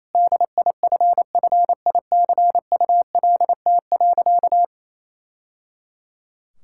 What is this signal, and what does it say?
Morse code, heard as a keyed tone, about 28 wpm, 700 Hz. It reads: DIFFICULT.
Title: Sentences from Top 1000 Words - Review Part 6 - 28wpm